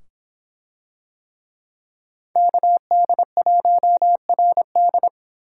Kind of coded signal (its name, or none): Morse code